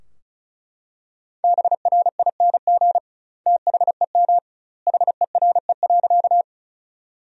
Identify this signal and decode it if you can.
Morse code, heard as a keyed tone, about 36 wpm, 700 Hz. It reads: BRING THEM HERE.
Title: 2x Word-Spacing - Sentences from Top 500 Words - 35wpm